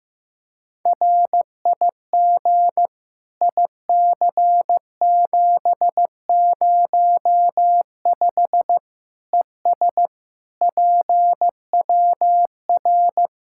Morse code, heard as a keyed tone, about 15 wpm, 700 Hz. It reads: RIG IC705 ES PWR
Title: Practice QSO - Exchange 2 - Info - 15wpm